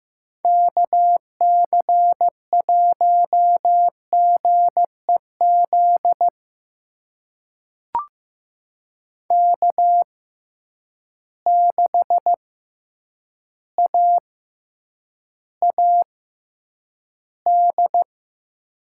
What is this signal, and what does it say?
Morse code, heard as a keyed tone, about 15 wpm, spaced as for 8 wpm, 700 Hz. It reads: KC1GEZ K6AAD
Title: USA Call Signs - 6x Character Spacing - 15wpm